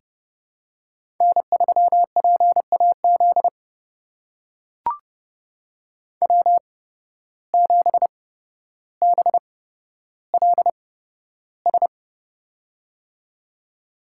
Morse code, heard as a keyed tone, about 30 wpm, spaced as for 13 wpm, 700 Hz. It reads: N3PAZ W7BLS
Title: USA Call Signs - 5 Characters - 8x Character Spacing - 30wpm